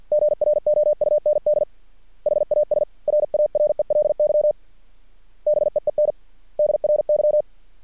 Morse code, heard as a keyed tone, about 50 wpm, 600 Hz. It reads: GROUND HAS RARELY BEEN DRY
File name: archived / CRSnet